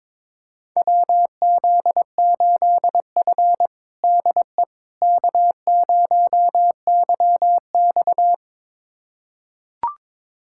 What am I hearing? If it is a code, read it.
Morse code, heard as a keyed tone, about 22 wpm, 700 Hz. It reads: WZ8F DE K0YX
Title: USA Callsign DE Callsign - 4 Characters - 22wpm